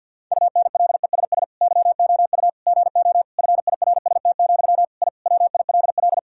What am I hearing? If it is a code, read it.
Morse code, heard as a keyed tone, about 50 wpm, 700 Hz. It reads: UNLESS YOU GO FIRST, I WILL
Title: Sentences from Top 1000 Words - Review Part 6 - 50wpm